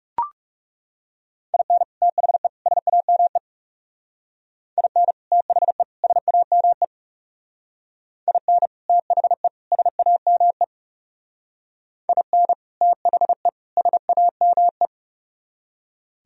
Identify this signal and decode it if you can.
Morse code, heard as a keyed tone, about 38 wpm, 700 Hz. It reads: IN THE SAME IN THE SAME IN THE SAME IN THE SAME